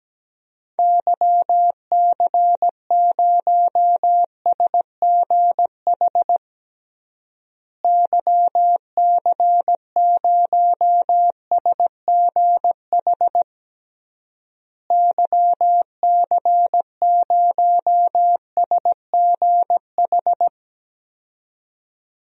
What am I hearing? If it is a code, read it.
Morse code, heard as a keyed tone, about 17 wpm, 700 Hz. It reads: YC0SGH YC0SGH YC0SGH
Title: DXCC-100 Call Signs - 6 Characters - Repeated 3x - 17wpm